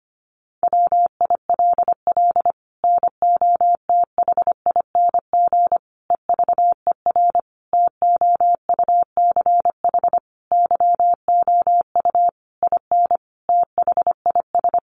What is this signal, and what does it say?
Morse code, heard as a keyed tone, about 25 wpm, 700 Hz. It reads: WILL NOTHING EVER TOUCH YOU IN THIS
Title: Sentences from Top 1200 Words 25wpm